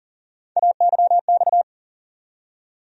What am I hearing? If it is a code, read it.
Morse code, heard as a keyed tone, about 40 wpm, 700 Hz. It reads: AYX
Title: USA Call Sign Postfixes - 3 Letters 40wpm